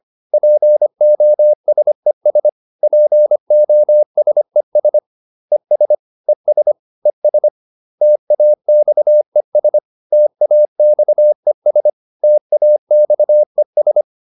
Morse code, heard as a keyed tone, about 25 wpm, 600 Hz. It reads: POSES POSES ES ES ES TAXES TAXES TAXES